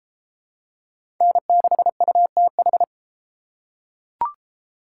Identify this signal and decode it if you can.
Morse code, heard as a keyed tone, about 30 wpm, 700 Hz. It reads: N6UTH